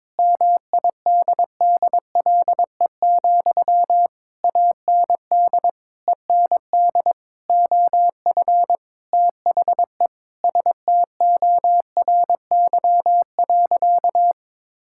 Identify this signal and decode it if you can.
Morse code, heard as a keyed tone, about 22 wpm, 700 Hz. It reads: MIDDLE, AND END OF THE STORY.
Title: Sentences from Top 1000 Words - Review Part 6 - 22wpm